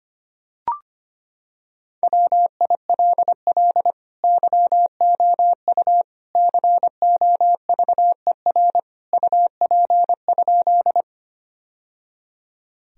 Morse code, heard as a keyed tone, about 25 wpm, 700 Hz. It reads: WILL YOU COVER UP?
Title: Sentences from Top 1000 Words - Review Part 1 - 25wpm